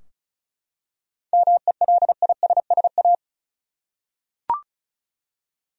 Morse code, heard as a keyed tone, about 36 wpm, 700 Hz. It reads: MELISSA
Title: USA Common Names - 35wpm